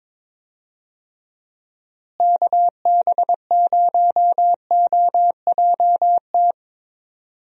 Morse code, heard as a keyed tone, about 22 wpm, 700 Hz. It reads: KB0OJT
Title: USA Call Signs - 6 Characters - Repeated 2x - 22wpm